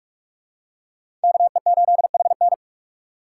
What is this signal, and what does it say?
Morse code, heard as a keyed tone, about 44 wpm, 700 Hz. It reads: KE8HN
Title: USA Call Signs - 5 Characters - 4x Character Spacing - 45wpm